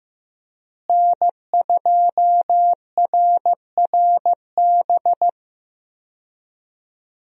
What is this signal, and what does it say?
Morse code, heard as a keyed tone, about 15 wpm, 700 Hz. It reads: N2RRB
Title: USA Call Signs - 5 Characters - Repeated 3x - 15wpm